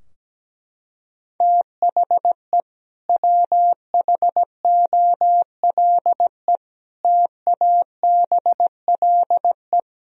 Morse code, heard as a keyed tone, about 17 wpm, 700 Hz. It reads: THE WHOLE TABLE
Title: Sentences from Top 1000 Words - Review Part 2 - 17wpm